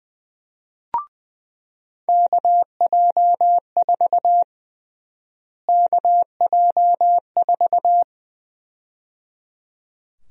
Morse code, heard as a keyed tone, about 20 wpm, 700 Hz. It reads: KJ4 KJ4
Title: USA Call Sign Prefixes - Repeated 2x - 20wpm